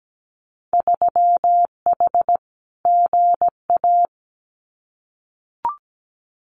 Morse code, heard as a keyed tone, about 17 wpm, 700 Hz. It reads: 3H GA
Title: Winter Field Day Exchanges Encore 17wpm